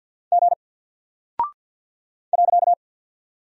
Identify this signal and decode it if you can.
Morse code, heard as a keyed tone, about 48 wpm, 700 Hz. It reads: G .
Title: Single Character (Letter:Number:Punctuation:Prosign) Alt 50wpm